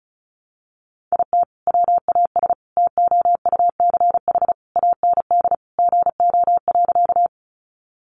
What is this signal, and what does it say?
Morse code, heard as a keyed tone, about 34 wpm, 700 Hz. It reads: IT WAS TOUCH AND GO.
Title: Sentences from Top 1200 Words 35wpm